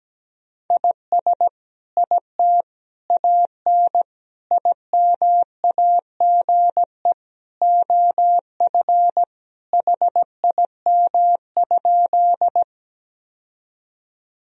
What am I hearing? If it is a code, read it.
Morse code, heard as a keyed tone, about 17 wpm, 700 Hz. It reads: IS IT AN IMAGE OF HIM?
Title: Sentences from Top 1000 Words - Review Part 4 - 17wpm